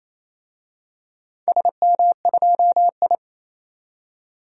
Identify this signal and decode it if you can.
Morse code, heard as a keyed tone, about 27 wpm, 700 Hz. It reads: SM2I
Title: DXCC-100 Call Signs - 4 Characters - 28wpm